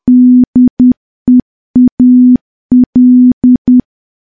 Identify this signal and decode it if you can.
Morse code, heard as a keyed tone, about 10 wpm, 250 Hz. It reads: DEAL
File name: morse / morse-training-10WPM